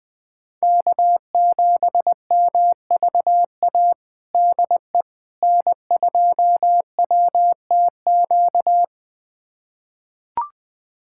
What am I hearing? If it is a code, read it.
Morse code, heard as a keyed tone, about 20 wpm, 700 Hz. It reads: K7MVA DE N2WTQ
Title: USA Callsign DE Callsign - 5 Characters - 20wpm